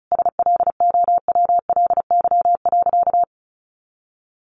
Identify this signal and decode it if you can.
Morse code, heard as a keyed tone, about 34 wpm, 700 Hz. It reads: SLOWLY.
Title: Sentences from Top 1100 Words 35wpm